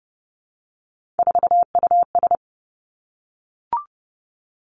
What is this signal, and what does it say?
Morse code, heard as a keyed tone, about 30 wpm, 700 Hz. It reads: 4US
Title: Three Challenging Letter-Number Combinations 30wpm